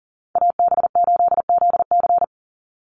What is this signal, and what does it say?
Morse code, heard as a keyed tone, about 40 wpm, 700 Hz. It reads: AB8ZC
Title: USA Call Signs 40wpm